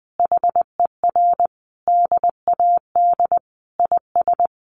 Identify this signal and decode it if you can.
Morse code, heard as a keyed tone, about 20 wpm, 700 Hz. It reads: HER DAD IS
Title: Additional Easy Phrases 20wpm